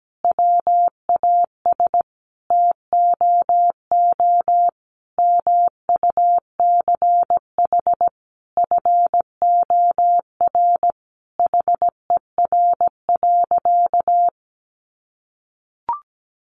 Morse code, heard as a keyed tone, about 17 wpm, 700 Hz. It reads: WAS TOO MUCH FOR HER.
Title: Sentences from Top 1200 Words 17wpm